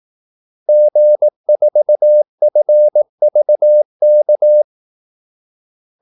Morse code, heard as a keyed tone, about 18 wpm, 600 Hz. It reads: G4FVK
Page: Kaldesignaler 18 wpm